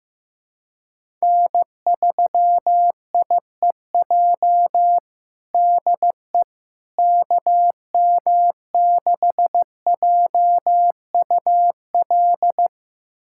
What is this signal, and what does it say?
Morse code, heard as a keyed tone, about 15 wpm, 700 Hz. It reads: N3IEJ DE KM6JUL
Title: USA Callsign DE Callsign 15wpm